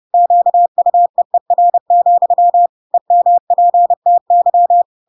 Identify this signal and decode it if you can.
Morse code, heard as a keyed tone, about 30 wpm, 700 Hz. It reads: QUEER, EMPTY